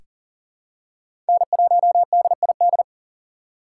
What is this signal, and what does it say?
Morse code, heard as a keyed tone, about 40 wpm, 700 Hz. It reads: N1DID
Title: K1USN SST QSO Practice - 40wpm